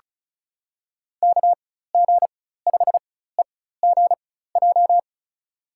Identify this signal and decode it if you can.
Morse code, heard as a keyed tone, about 34 wpm, spaced as for 16 wpm, 700 Hz. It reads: KG5EGJ
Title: USA Call Signs - 6 Characters - 4x Character Spacing - 35wpm